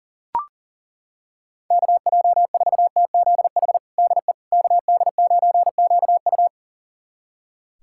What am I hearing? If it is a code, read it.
Morse code, heard as a keyed tone, about 40 wpm, 700 Hz. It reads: KJ4TZH DE KD9QU
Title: USA Callsign DE Callsign 40wpm